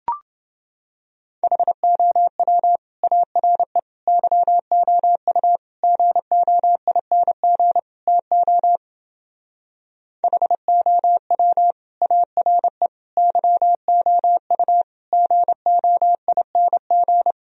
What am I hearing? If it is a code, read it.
Morse code, heard as a keyed tone, about 28 wpm, 700 Hz. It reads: HOW ARE YOU GOING TO HOW ARE YOU GOING